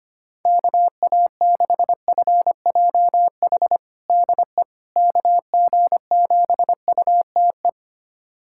Morse code, heard as a keyed tone, about 25 wpm, 700 Hz. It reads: KA6FJH DE KG7UTE
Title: USA Callsign DE Callsign - 6 Characters - 25wpm